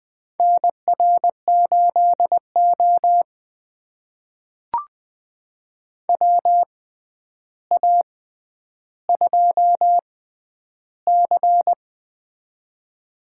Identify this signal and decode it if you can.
Morse code, heard as a keyed tone, about 20 wpm, spaced as for 10 wpm, 700 Hz. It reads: NR8O WA2C
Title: USA Call Signs - 4 Characters - 6x Character Spacing - 20wpm